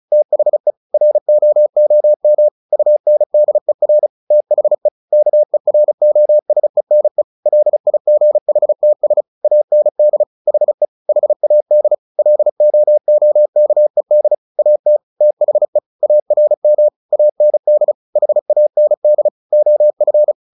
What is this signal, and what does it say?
Morse code, heard as a keyed tone, about 34 wpm, 600 Hz. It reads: THE ROOM UNDER THE KEROSENE LIGHTS AND HE HAD LOOKED AT THE ARM AND HAND OF
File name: the old man and the sea 2 35wpm